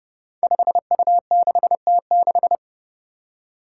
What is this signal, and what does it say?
Morse code, heard as a keyed tone, about 31 wpm, 700 Hz. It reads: 5U6T6
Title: Five Challenging Letter-Number Combinations 30wpm